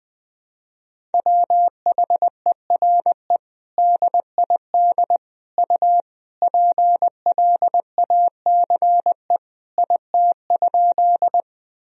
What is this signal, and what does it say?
Morse code, heard as a keyed tone, about 20 wpm, 700 Hz. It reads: WHERE DID U PLACE IT?
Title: Sentences from Top 200 Words and Abbreviations 20wpm